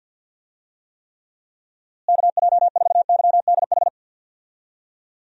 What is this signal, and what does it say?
Morse code, heard as a keyed tone, about 52 wpm, 700 Hz. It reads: KJ4YDH
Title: USA Call Signs - 6 Characters - Repeated 2x - 50wpm